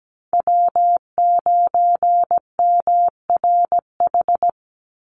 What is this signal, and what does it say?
Morse code, heard as a keyed tone, about 17 wpm, 700 Hz. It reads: W9MRH
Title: NA SOTA Chaser Callsigns 17wpm